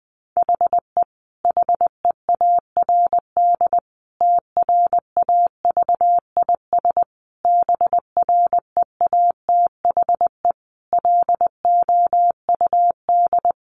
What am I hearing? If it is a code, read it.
Morse code, heard as a keyed tone, about 20 wpm, 700 Hz. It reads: HE HEARD TRAVIS BREATHE LOUD
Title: A Sound Of Thunder 20wpm